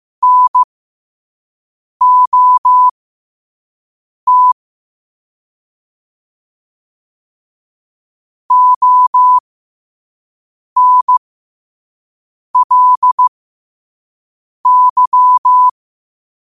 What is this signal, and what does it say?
Morse code, heard as a keyed tone, about 15 wpm, spaced as for 5 wpm, 1000 Hz. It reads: NOT ONLY
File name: 5wpm